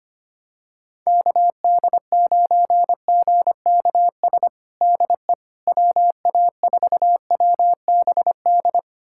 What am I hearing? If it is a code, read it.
Morse code, heard as a keyed tone, about 25 wpm, 700 Hz. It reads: KD9GKS DE WA4WBD
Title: USA Callsign DE Callsign - 6 Characters - 25wpm